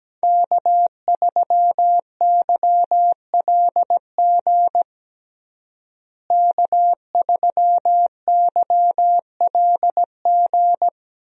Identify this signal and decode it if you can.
Morse code, heard as a keyed tone, about 17 wpm, 700 Hz. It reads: K3YLG K3YLG
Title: USA Call Signs - 5 Characters - Repeated 2x - 17wpm